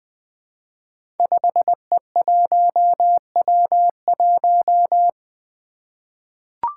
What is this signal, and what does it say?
Morse code, heard as a keyed tone, about 20 wpm, 700 Hz. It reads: 5E1W1
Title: Five Challenging Letter-Number Combinations 20wpm